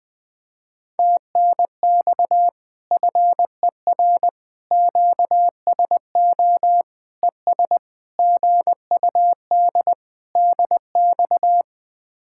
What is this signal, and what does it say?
Morse code, heard as a keyed tone, about 20 wpm, 700 Hz. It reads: TNX FER QSO ES GUD DX
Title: Speed-Racing Common QSO Phrases Encore - 17wpm